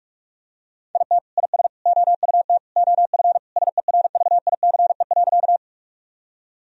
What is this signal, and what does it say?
Morse code, heard as a keyed tone, about 41 wpm, 700 Hz. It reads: IT IS OUT OF SERVICE.
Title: Sentences from Top 1000 Words - Review Part 4 - 45wpm